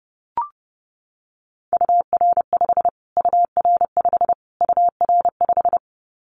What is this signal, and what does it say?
Morse code, heard as a keyed tone, about 28 wpm, 700 Hz. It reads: UR5 UR5 UR5
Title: Callsign Prefixes for Top 25 DXCC - Repeated 3x - 30wpm